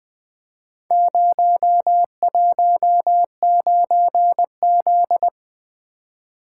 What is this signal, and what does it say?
Morse code, heard as a keyed tone, about 21 wpm, 700 Hz. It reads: 019Z
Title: Four Letter-Number Combinations - 20wpm